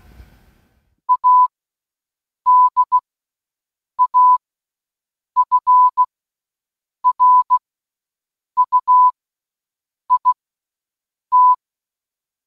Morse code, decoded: ADAFRUIT